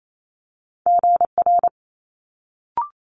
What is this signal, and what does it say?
Morse code, heard as a keyed tone, about 28 wpm, 700 Hz. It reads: GR
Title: Multi-letter Phonemes Quick Fire 28wpm